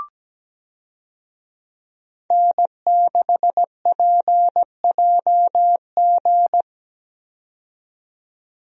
Morse code, heard as a keyed tone, about 17 wpm, 700 Hz. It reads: N6PJG
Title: USA Call Signs - 5 Characters - 17wpm